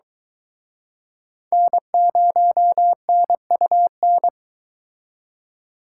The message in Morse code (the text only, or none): N0NUN